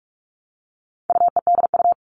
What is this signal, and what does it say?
Morse code, heard as a keyed tone, about 44 wpm, 700 Hz. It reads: UEDU
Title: All Cut Numbers - 4-Digit 45wpm